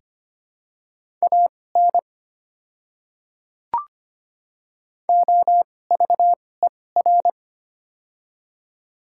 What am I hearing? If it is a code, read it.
Morse code, heard as a keyed tone, about 25 wpm, spaced as for 18 wpm, 700 Hz. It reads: AN OVER
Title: 2x Character Spacing - Top 100 Words - 25wpm